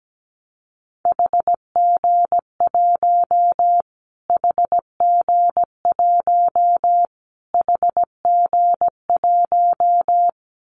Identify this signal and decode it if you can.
Morse code, heard as a keyed tone, about 17 wpm, 700 Hz. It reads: HG1 HG1 HG1
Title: Callsign Prefixes for Top 25 DXCC - Repeated 3x - 17wpm